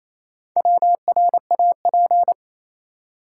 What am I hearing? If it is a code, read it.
Morse code, heard as a keyed tone, about 28 wpm, 700 Hz. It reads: WRAP
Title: Four Letter Words - 28wpm